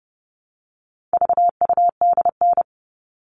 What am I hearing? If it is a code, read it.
Morse code, heard as a keyed tone, about 30 wpm, 700 Hz. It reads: VUDN
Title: All Cut Numbers - 4-Digit 30wpm